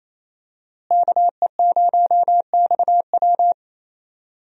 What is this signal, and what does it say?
Morse code, heard as a keyed tone, about 27 wpm, 700 Hz. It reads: KE0XW
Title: USA Call Signs - 5 Characters - Repeated 2x - 28wpm